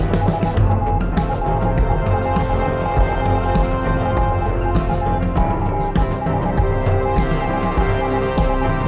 Morse code, heard as a keyed tone, about 20 wpm, 820 Hz. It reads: SAFELY REACH THIS